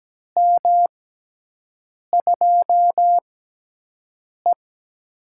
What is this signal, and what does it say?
Morse code, heard as a keyed tone, about 17 wpm, spaced as for 6 wpm, 700 Hz. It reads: M2E
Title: USA Call Signs - 6x Character Spacing - 17wpm